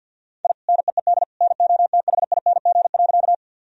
Morse code, heard as a keyed tone, about 52 wpm, 700 Hz. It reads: I NEED NOTHING.
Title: Sentences from Top 1000 Words - Review Part 1 - 50wpm